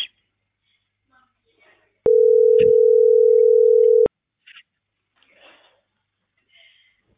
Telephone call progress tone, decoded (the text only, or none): ringback tone